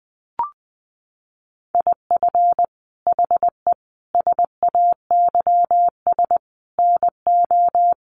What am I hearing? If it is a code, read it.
Morse code, heard as a keyed tone, about 20 wpm, 700 Hz. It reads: IF HE SAYS NO
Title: A Sound Of Thunder 20wpm